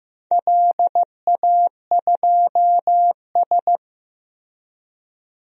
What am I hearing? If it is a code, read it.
Morse code, heard as a keyed tone, about 15 wpm, 700 Hz. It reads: LA2S
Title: DXCC-100 Call Signs - 4 Characters - Repeated 3x - 15wpm